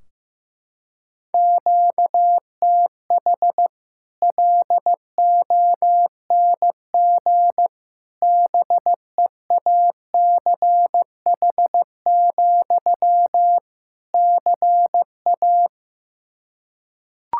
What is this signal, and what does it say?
Morse code, heard as a keyed tone, about 16 wpm, 700 Hz. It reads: QTH LONG BEACH, CA
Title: Practice QSO - Exchange 1 - Beginning - 15wpm